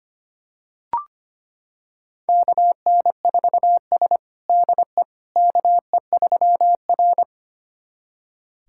Morse code, heard as a keyed tone, about 25 wpm, 700 Hz. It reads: KN4S DE KE3R
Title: USA Callsign DE Callsign - 4 Characters - 25wpm